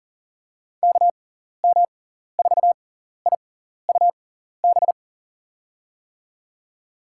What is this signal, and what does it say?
Morse code, heard as a keyed tone, about 37 wpm, spaced as for 14 wpm, 700 Hz. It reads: KM4IUB
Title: USA Call Signs - 6 Characters - 6x Character Spacing - 40wpm